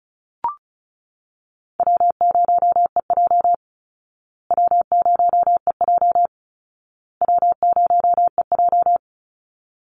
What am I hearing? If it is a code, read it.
Morse code, heard as a keyed tone, about 34 wpm, 700 Hz. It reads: W0EJ W0EJ W0EJ
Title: Easy Call Signs Repeated 3x - 35wpm